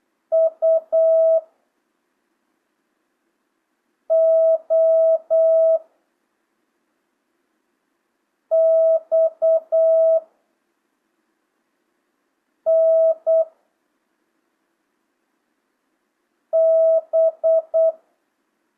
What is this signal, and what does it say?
Morse code, heard as a keyed tone, about 8 wpm, 650 Hz. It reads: U O X N B
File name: jh8cbh / 2025cw